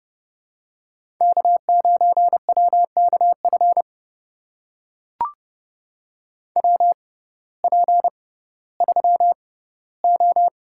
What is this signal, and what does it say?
Morse code, heard as a keyed tone, about 28 wpm, spaced as for 16 wpm, 700 Hz. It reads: K9WKF WP3O
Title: USA Call Signs - 5 Characters - 6x Character Spacing - 30wpm